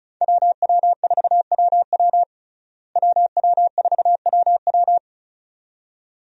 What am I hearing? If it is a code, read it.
Morse code, heard as a keyed tone, about 34 wpm, 700 Hz. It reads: WW4WW WW4WW
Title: USA Call Signs - 5 Characters - Repeated 2x - 35wpm